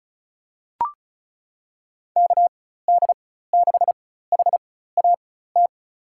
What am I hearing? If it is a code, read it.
Morse code, heard as a keyed tone, about 34 wpm, spaced as for 16 wpm, 700 Hz. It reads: KD6HAT
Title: USA Call Signs - 4x Character Spacing - 35wpm